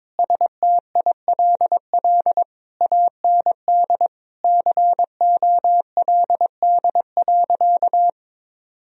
Morse code, heard as a keyed tone, about 22 wpm, 700 Hz. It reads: STILL AND COLD.
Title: Sentences from Top 1000 Words - Review Part 2 - 22wpm